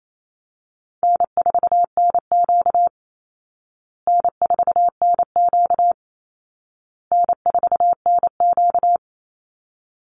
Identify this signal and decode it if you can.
Morse code, heard as a keyed tone, about 28 wpm, 700 Hz. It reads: N4NQ N4NQ N4NQ
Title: Easy Call Signs Repeated 3x - 28wpm